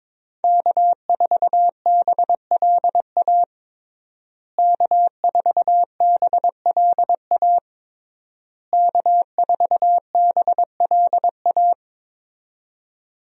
Morse code, heard as a keyed tone, about 22 wpm, 700 Hz. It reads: K4BLA K4BLA K4BLA
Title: USA Call Signs - 5 Characters - Repeated 3x - 22wpm